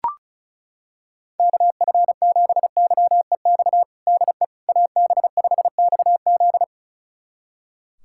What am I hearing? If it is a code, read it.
Morse code, heard as a keyed tone, about 34 wpm, 700 Hz. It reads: KF7YEX DE AB5XZ
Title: USA Callsign DE Callsign 35wpm